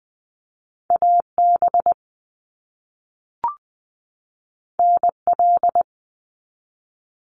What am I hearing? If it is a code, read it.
Morse code, heard as a keyed tone, about 20 wpm, 700 Hz. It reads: AB NL
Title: Canadian Provinces and Territories 20wpm